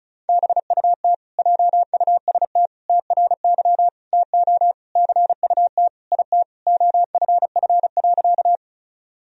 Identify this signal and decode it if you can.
Morse code, heard as a keyed tone, about 36 wpm, 700 Hz. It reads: BUT JUST TRY TO CUT IT OFF.
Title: Sentences from Top 1000 Words - Review Part 1 - 35wpm